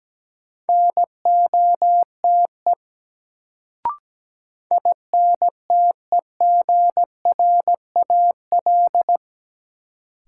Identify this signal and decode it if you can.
Morse code, heard as a keyed tone, about 17 wpm, 700 Hz. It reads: NOTE INTEGRAL
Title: Sets of 5 Words - New Word Drill - 17wpm